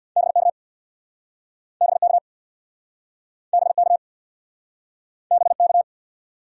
Morse code, heard as a keyed tone, about 68 wpm, 700 Hz. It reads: BK BK BK BK